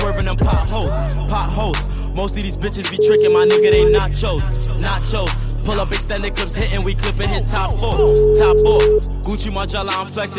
Ringback tone (Europe, 425 Hz), with noise.